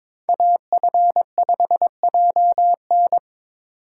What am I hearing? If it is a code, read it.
Morse code, heard as a keyed tone, about 22 wpm, 700 Hz. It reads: AF5JN
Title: USA Call Signs - 6x Character Spacing - 22wpm